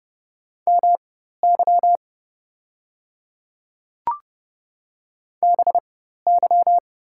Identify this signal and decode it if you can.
Morse code, heard as a keyed tone, about 30 wpm, spaced as for 14 wpm, 700 Hz. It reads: MY BY
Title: 4x Character Spacing - Top 200 Words - 30wpm